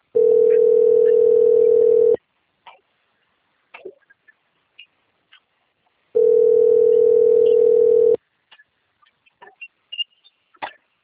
Ringback tone (North America), 440 + 480 Hz.
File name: out- - - - - .10821